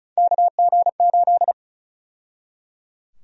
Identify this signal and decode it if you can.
Morse code, heard as a keyed tone, about 34 wpm, 700 Hz. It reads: KG8
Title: USA Call Sign Prefixes - 35wpm